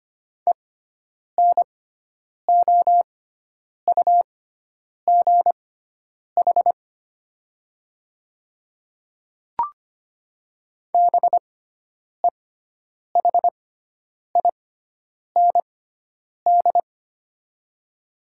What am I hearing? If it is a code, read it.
Morse code, heard as a keyed tone, about 25 wpm, spaced as for 9 wpm, 690 Hz. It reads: ENOUGH BEHIND
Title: 6x Character Spacing - Top 500 Words - 25wpm